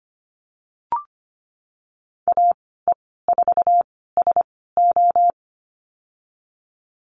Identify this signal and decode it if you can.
Morse code, heard as a keyed tone, about 25 wpm, spaced as for 16 wpm, 700 Hz. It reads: AE4SO
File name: Lesson 15g - Number 5 - Call Signs 25wpm